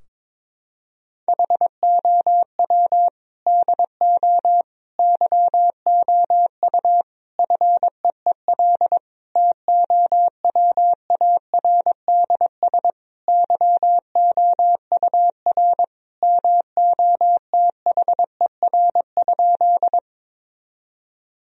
Morse code, heard as a keyed tone, about 22 wpm, 700 Hz. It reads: HOW DO YOU FEEL TOWARDS YOUR MOTHER?
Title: Sentences from Top 1000 Words - Review Part 2 - 22wpm